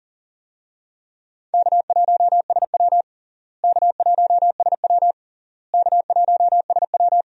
Morse code, heard as a keyed tone, about 40 wpm, 700 Hz. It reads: K1SW K1SW K1SW
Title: USA Call Signs - 4 Characters - Repeated 3x - 40wpm